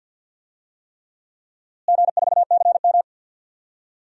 Morse code, heard as a keyed tone, about 50 wpm, 700 Hz. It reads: G4CM
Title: DXCC-100 Call Signs - 4 Characters - Repeated 3x - 50wpm